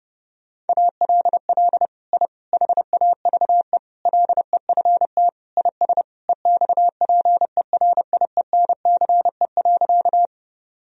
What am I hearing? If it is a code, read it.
Morse code, heard as a keyed tone, about 30 wpm, 700 Hz. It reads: ALL I HAVE LEFT IS EXPERIENCE.
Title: Sentences from Top 1000 Words - Review Part 3 - 30wpm